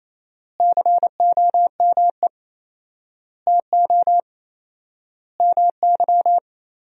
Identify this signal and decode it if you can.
Morse code, heard as a keyed tone, about 28 wpm, 700 Hz. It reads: COME TO MY
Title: 4x Word-Spacing - Sentences from Top 400 Words - 28wpm